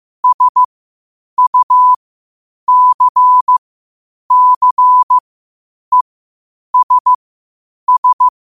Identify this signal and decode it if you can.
Morse code, heard as a keyed tone, about 15 wpm, spaced as for 8 wpm, 1000 Hz. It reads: SUCCESS
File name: qotdfiles / 8wpm